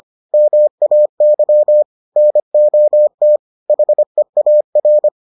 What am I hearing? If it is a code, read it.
Morse code, heard as a keyed tone, about 25 wpm, 600 Hz. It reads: MAY NOT HEAR